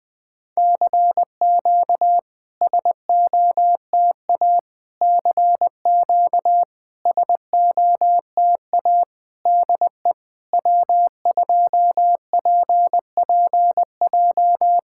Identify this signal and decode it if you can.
Morse code, heard as a keyed tone, about 20 wpm, 700 Hz. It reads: CQ SOTA CQ SOTA DE W2PPJ
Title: SOTA QSO Practice 20wpm